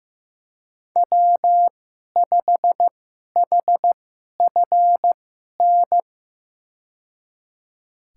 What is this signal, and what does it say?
Morse code, heard as a keyed tone, about 15 wpm, spaced as for 11 wpm, 700 Hz. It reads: W5HFN
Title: USA Call Signs - 2x Character Spacing - 15wpm